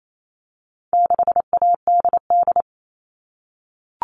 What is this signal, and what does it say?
Morse code, heard as a keyed tone, about 28 wpm, 700 Hz. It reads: 6ADD
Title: All Cut Numbers - 4-Digit 28wpm